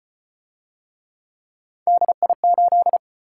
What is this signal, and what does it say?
Morse code, heard as a keyed tone, about 33 wpm, 700 Hz. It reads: DI8